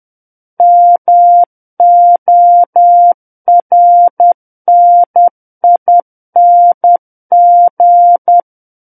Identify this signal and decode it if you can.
Morse code, heard as a keyed tone, about 10 wpm, 700 Hz. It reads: MORNING
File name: Book One / 01-Chapter 1 - 10 WPM